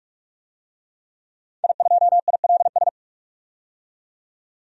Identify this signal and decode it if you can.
Morse code, heard as a keyed tone, about 41 wpm, 700 Hz. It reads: I2ILS
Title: DXCC-100 Call Signs - 5 Characters - 45wpm